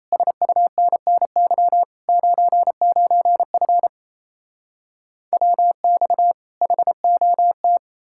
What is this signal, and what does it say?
Morse code, heard as a keyed tone, about 38 wpm, 700 Hz. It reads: SUNNY 99F WX HOT